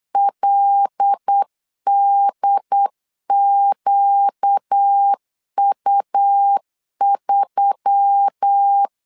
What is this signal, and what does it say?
Morse code, heard as a keyed tone, about 8 wpm, 800 Hz. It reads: LDQU3
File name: LEZIONI / 6 PKX0 / 6 P 8wpm 3025